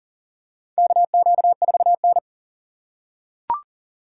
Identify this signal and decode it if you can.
Morse code, heard as a keyed tone, about 40 wpm, 700 Hz. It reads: KQ4N
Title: USA Call Signs - 4 Characters - 2x Character Spacing - 40wpm